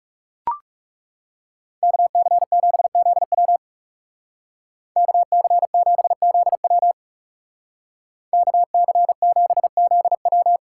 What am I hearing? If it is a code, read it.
Morse code, heard as a keyed tone, about 36 wpm, 700 Hz. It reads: KC7ZW KC7ZW KC7ZW
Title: Speed Racing - USA 2x2 Call Signs 30wpm